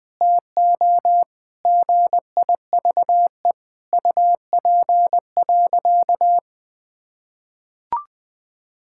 Morse code, heard as a keyed tone, about 20 wpm, 700 Hz. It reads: TO GIVE UP.
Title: Sentences with 5-Word Phrases - 20wpm